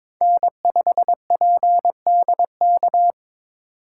Morse code, heard as a keyed tone, about 22 wpm, 700 Hz. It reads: N5PDK
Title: USA Call Signs - 5 Characters - 2x Character Spacing - 22wpm